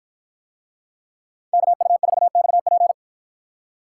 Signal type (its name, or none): Morse code